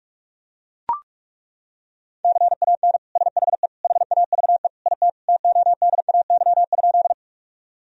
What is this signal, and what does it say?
Morse code, heard as a keyed tone, about 41 wpm, 700 Hz. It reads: CAN SHE HAVE IT TODAY?
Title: Sentences from Top 1000 Words - Review Part 4 - 45wpm